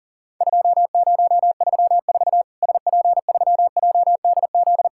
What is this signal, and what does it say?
Morse code, heard as a keyed tone, about 40 wpm, 700 Hz. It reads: 2034 SP3JDZ
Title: Worked All Europe DX Contest - QTC Exchange - 40wpm